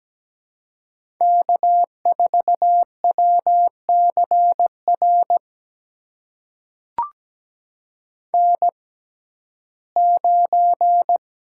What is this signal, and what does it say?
Morse code, heard as a keyed tone, about 17 wpm, spaced as for 12 wpm, 700 Hz. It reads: K4WCR N9
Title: USA Call Signs - 5 Characters - 6x Character Spacing - 17wpm